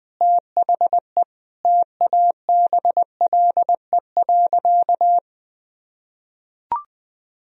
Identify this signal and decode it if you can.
Morse code, heard as a keyed tone, about 20 wpm, 700 Hz. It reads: THE TABLE.
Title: Sentences from Top 1000 Words - Review Part 2 - 20wpm